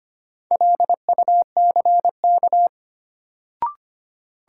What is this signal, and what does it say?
Morse code, heard as a keyed tone, about 25 wpm, 700 Hz. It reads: LUCK
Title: Sentences from Top 1500 Words - New Word Drill - 25wpm